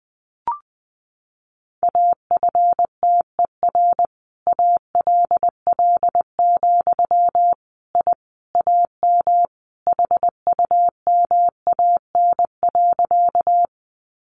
Morse code, heard as a keyed tone, about 19 wpm, 700 Hz. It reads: AFTER ALL, I AM HUMAN.
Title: Sentences from Top 600 Words 20wpm